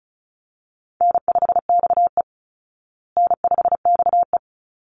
Morse code, heard as a keyed tone, about 34 wpm, 700 Hz. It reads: N5XE N5XE
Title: Easy Call Signs Repeated 2x - 35wpm